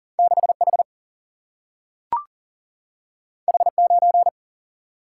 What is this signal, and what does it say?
Morse code, heard as a keyed tone, about 39 wpm, 700 Hz. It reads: 6H H9
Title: Single-Letter Single-Number 40wpm